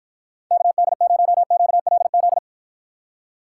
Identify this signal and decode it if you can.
Morse code, heard as a keyed tone, about 53 wpm, 700 Hz. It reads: KD0QLZ